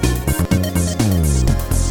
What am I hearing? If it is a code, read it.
There is no code.